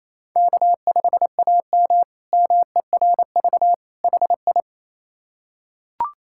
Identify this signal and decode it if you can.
Morse code, heard as a keyed tone, about 27 wpm, 700 Hz. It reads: K5AM MERV HI
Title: K1USN SST QSO Practice - 28wpm